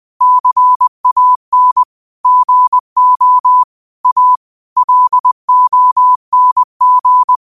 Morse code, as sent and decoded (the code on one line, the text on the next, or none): -.-. .- -. / --. --- / .- / .-.. --- -. --.
CAN GO A LONG